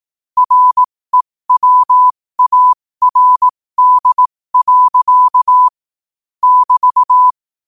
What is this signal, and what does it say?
Morse code, heard as a keyed tone, about 18 wpm, spaced as for 15 wpm, 1000 Hz. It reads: REWARD. <BT>
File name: qotdfiles / 15wpm